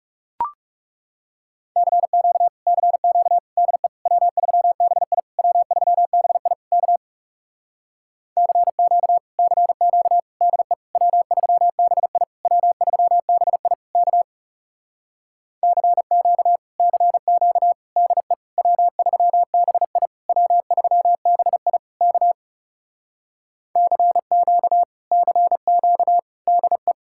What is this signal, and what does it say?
Morse code, heard as a keyed tone, about 38 wpm, 700 Hz. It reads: CQ CQ DE W3BI W3BI K CQ CQ DE W3BI W3BI K CQ CQ DE W3BI W3BI K CQ CQ DE
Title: Speed-Racing Common QSO Phrases Encore - 30wpm